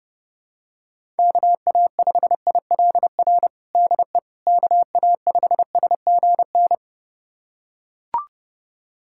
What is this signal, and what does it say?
Morse code, heard as a keyed tone, about 30 wpm, 700 Hz. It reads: KA5ILR DE KA5SGN
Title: USA Callsign DE Callsign - 6 Characters - 30wpm